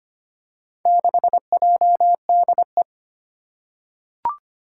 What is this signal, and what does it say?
Morse code, heard as a keyed tone, about 25 wpm, 700 Hz. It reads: 6JDE
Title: Four Challenging Letter-Number Combinations 25wpm